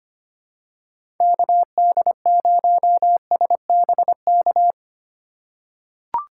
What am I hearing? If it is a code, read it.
Morse code, heard as a keyed tone, about 25 wpm, 700 Hz. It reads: KD0SBK
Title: USA Call Signs - 3x Character Spacing - 25wpm